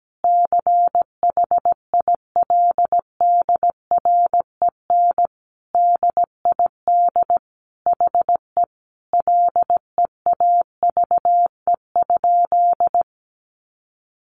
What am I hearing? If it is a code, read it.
Morse code, heard as a keyed tone, about 17 wpm, 700 Hz. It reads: CHILDREN DID HE LEAVE?
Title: Sentences from Top 400 Words 17wpm